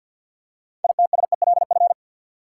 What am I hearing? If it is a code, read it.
Morse code, heard as a keyed tone, about 47 wpm, 700 Hz. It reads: ITSELF
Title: Six Letter Words - 50wpm